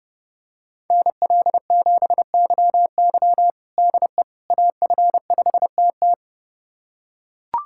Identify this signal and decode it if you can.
Morse code, heard as a keyed tone, about 32 wpm, 700 Hz. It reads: NL7YY DE AF5TT
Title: USA Callsign DE Callsign - 5 Characters - 30wpm